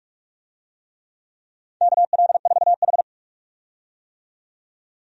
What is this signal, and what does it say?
Morse code, heard as a keyed tone, about 42 wpm, 700 Hz. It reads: KL4H
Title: USA Call Signs - 4 Characters - 45wpm